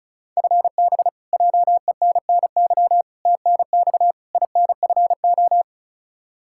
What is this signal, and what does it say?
Morse code, heard as a keyed tone, about 36 wpm, 700 Hz. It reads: FB JENNY TNX INFO
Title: K1USN SST QSO Practice 35wpm